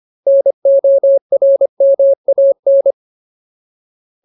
Morse code, heard as a keyed tone, about 25 wpm, 550 Hz. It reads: NORMAN
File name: Ultimate Word List / 25WPM Ultimate